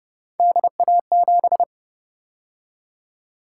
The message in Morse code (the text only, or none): DA7